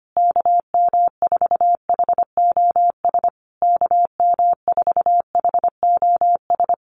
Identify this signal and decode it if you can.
Morse code, heard as a keyed tone, about 25 wpm, 700 Hz. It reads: KM4HOS KM4HOS
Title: USA Call Signs Repeated 3x 25wpm